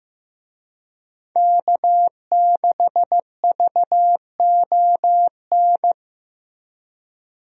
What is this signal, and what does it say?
Morse code, heard as a keyed tone, about 15 wpm, 700 Hz. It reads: K6VON
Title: USA Call Signs - 5 Characters - Repeated 3x - 15wpm